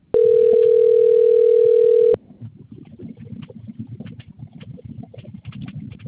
Ringback tone (North America), 440 + 480 Hz.